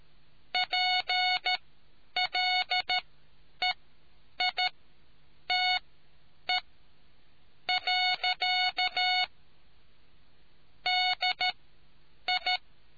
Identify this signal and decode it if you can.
Morse code, heard as a keyed tone, about 13 wpm, spaced as for 8 wpm, 730 Hz. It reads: PLEITE. DI